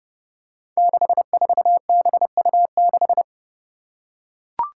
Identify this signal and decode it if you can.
Morse code, heard as a keyed tone, about 32 wpm, 700 Hz. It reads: 64BU6
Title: Five Challenging Letter-Number Combinations 30wpm